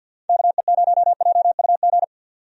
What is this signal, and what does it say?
Morse code, heard as a keyed tone, about 52 wpm, 700 Hz. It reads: KE0JUG
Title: USA Call Signs - 6 Characters - 50wpm